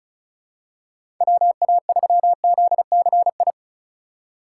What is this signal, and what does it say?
Morse code, heard as a keyed tone, about 34 wpm, 700 Hz. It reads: WA3ZCI
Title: USA Call Signs - 6 Characters - 3x Character Spacing - 35wpm